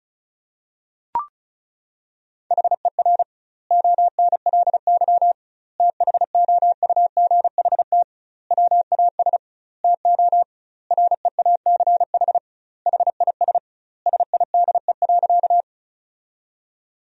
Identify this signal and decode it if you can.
Morse code, heard as a keyed tone, about 34 wpm, 700 Hz. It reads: HER ONLY THOUGHT WAS TO REACH HIS SIDE.
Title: 2x Word-Spacing - Sentences from Top 400 Words - 35wpm